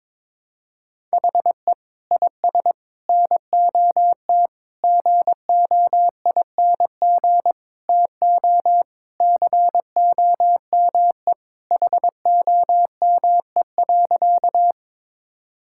Morse code, heard as a keyed tone, about 22 wpm, 700 Hz. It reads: HE IS NOT GOING TO COME HOME.
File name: practice / 22wpm / Sentences with 4-Word Phrases - 22wpm